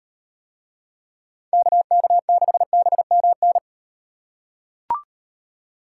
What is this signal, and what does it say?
Morse code, heard as a keyed tone, about 38 wpm, 700 Hz. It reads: KK6BMN